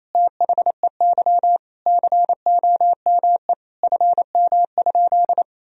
Morse code, heard as a keyed tone, about 28 wpm, 700 Hz. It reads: THEY COME FM?
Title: Sentences from Top 200 Words and Abbreviations 28wpm